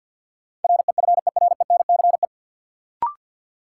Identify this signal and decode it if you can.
Morse code, heard as a keyed tone, about 53 wpm, 700 Hz. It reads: REFERENCE
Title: Sentences from Top 1200 Words - New Word Drill Alt - 50wpm